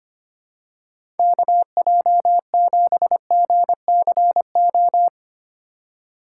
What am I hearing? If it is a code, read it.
Morse code, heard as a keyed tone, about 25 wpm, 700 Hz. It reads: KJ7GCO
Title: USA Call Signs - 4x Character Spacing - 25wpm